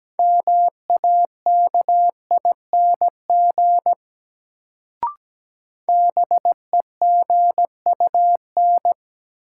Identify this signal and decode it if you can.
Morse code, heard as a keyed tone, about 17 wpm, 700 Hz. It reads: MAKING BEGUN
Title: Sentences from Top 1000 Words - Review Part 6 - New Word Drill - 17wpm